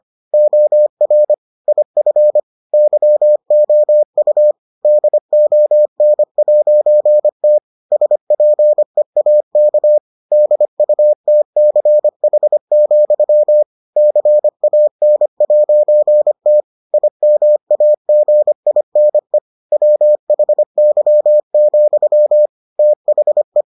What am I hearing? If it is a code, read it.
Morse code, heard as a keyed tone, about 25 wpm, 600 Hz. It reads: OR IF YOU DON'T SPEAK DUTCH, CAN'T IMAGINE WHY, THE